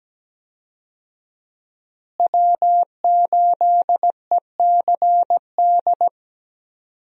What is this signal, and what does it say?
Morse code, heard as a keyed tone, about 17 wpm, 700 Hz. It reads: W8ECD
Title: USA Call Signs - 5 Characters - Repeated 3x - 17wpm